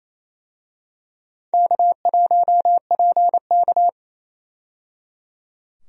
Morse code, heard as a keyed tone, about 28 wpm, 700 Hz. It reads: K1PK
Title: USA Call Signs - 4 Characters - 28wpm